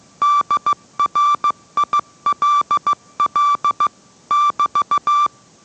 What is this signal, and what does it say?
Morse code, heard as a keyed tone, about 15 wpm, 1170 Hz. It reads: DRILL <BT>